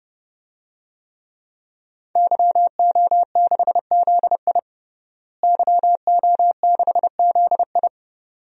Morse code, heard as a keyed tone, about 30 wpm, 700 Hz. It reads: YO6ZI YO6ZI
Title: DXCC-100 Call Signs - 5 Characters - Repeated 2x - 30wpm